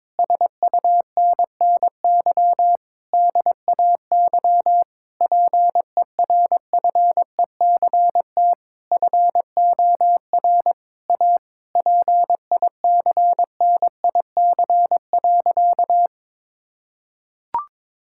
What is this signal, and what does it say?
Morse code, heard as a keyed tone, about 22 wpm, 700 Hz. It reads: SUNNY DAY PERFECT FOR A PICNIC.
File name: Difficult Sentences with Hints - 22wpm